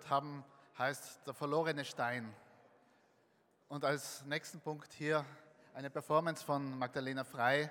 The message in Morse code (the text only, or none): none